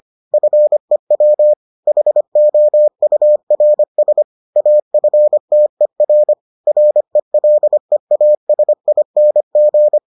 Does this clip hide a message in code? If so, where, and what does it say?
Yes: Morse code from 0.3 to 10.0 s, heard as a keyed tone, about 25 wpm, 600 Hz. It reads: FEW HOURS AFTER RELEASING